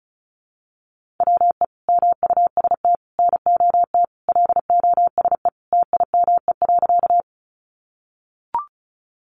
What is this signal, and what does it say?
Morse code, heard as a keyed tone, about 34 wpm, 700 Hz. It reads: WE MUST NOT LOSE TIME.
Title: Sentences from Top 1100 Words 35wpm